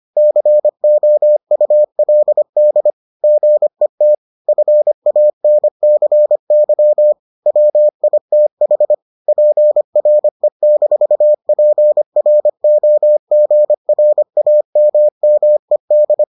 Morse code, heard as a keyed tone, about 25 wpm, 600 Hz. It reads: COULD GET FANCY WITH PRE-PROGRAMMED